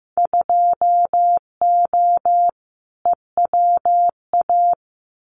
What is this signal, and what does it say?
Morse code, heard as a keyed tone, about 15 wpm, 700 Hz. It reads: 2O EWA
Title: Winter Field Day Exchanges Encore 15wpm